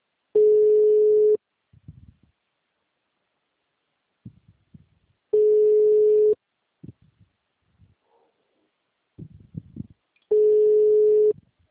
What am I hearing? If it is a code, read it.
Ringback tone (Europe, 425 Hz).